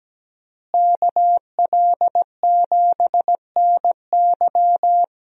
Morse code, heard as a keyed tone, about 17 wpm, 700 Hz. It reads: KL7NY
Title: USA Call Signs - 5 Characters - Repeated 3x - 17wpm